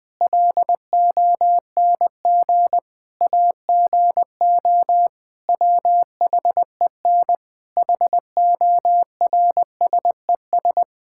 Morse code, heard as a keyed tone, about 20 wpm, 700 Hz. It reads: LONG AGO WHEN HORSES
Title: Sentences with 8-Word Phrases - Spoken Beginning - 20wpm